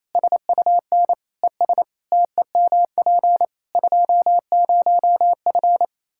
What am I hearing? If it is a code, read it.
Morse code, heard as a keyed tone, about 28 wpm, 700 Hz. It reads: SUN ES TEMP 20F
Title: Practice QSO - Exchange 2 - Info - 28wpm